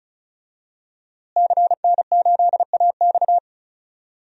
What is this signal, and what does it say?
Morse code, heard as a keyed tone, about 34 wpm, 700 Hz. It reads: CN8AX
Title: DXCC-100 Call Signs - 5 Characters - Repeated 2x - 35wpm